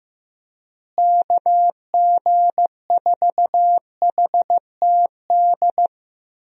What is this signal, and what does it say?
Morse code, heard as a keyed tone, about 15 wpm, 700 Hz. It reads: KG4HTD
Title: USA Call Signs - 6 Characters - Repeated 2x - 15wpm